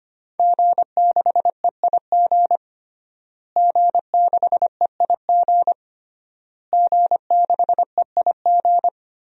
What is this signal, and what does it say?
Morse code, heard as a keyed tone, about 25 wpm, 700 Hz. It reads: G6EIG G6EIG G6EIG
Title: DXCC-100 Call Signs - 5 Characters - Repeated 3x - 25wpm